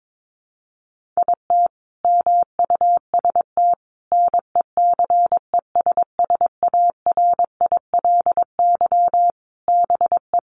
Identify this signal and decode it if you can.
Morse code, heard as a keyed tone, about 22 wpm, 700 Hz. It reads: IT MUST NECESSARILY BE
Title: Sentences from Top 1400 Words 22wpm